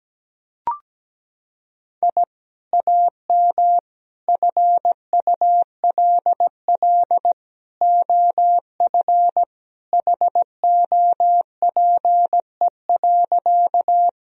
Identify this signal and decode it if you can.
Morse code, heard as a keyed tone, about 17 wpm, 700 Hz. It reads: I AM FULL OF HOPE.
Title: Sentences from Top 1000 Words - Review Part 3 - 17wpm